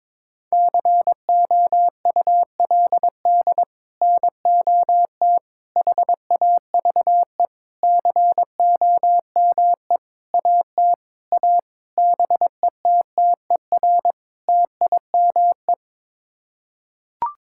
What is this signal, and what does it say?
Morse code, heard as a keyed tone, about 22 wpm, 700 Hz. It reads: COULD NOT HAVE COME AT A BETTER TIME
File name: practice / 22wpm / Sets of 8 Words - 22wpm